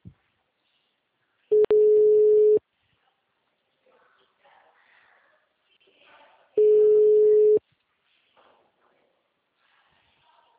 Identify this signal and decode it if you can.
Ringback tone (Europe, 425 Hz).